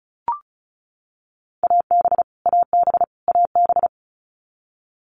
Morse code, heard as a keyed tone, about 34 wpm, 700 Hz. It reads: AB AB AB
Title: Canadian Provinces and Territories - Repeated 3x - 35wpm